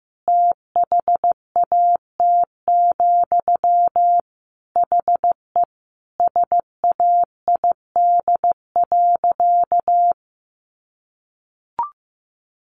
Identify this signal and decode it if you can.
Morse code, heard as a keyed tone, about 14 wpm, 700 Hz. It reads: THAT, HE SAID.
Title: Sentences from Top 1000 Words 15wpm